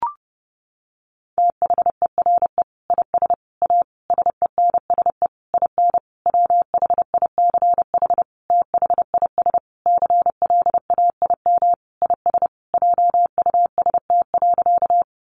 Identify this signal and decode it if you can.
Morse code, heard as a keyed tone, about 30 wpm, 700 Hz. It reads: THERE IS A SENSE IN WHICH THIS CLAIM IS JUST.
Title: Sentences from Top 1500 Words 30wpm